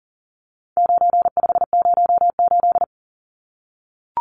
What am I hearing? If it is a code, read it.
Morse code, heard as a keyed tone, about 40 wpm, 700 Hz. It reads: 9508
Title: Numbers - 4-Digit 40wpm